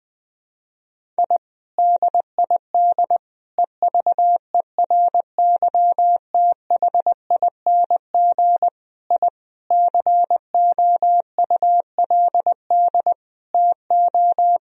Morse code, heard as a keyed tone, about 20 wpm, 700 Hz. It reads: I DID EVERYTHING I COULD TO
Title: Sentences from Top 1000 Words - Review Part 1 - 20wpm